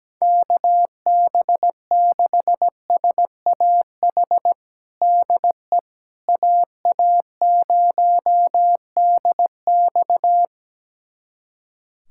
Morse code, heard as a keyed tone, about 17 wpm, 700 Hz. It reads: KB6SAH DE AA0DX
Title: USA Callsign DE Callsign 17wpm